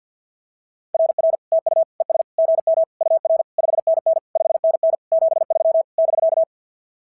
Morse code, heard as a keyed tone, about 46 wpm, 660 Hz. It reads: RR TU ES GM UR 5NN 5NN 73 <BK>
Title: SOTA QSO Practice 50wpm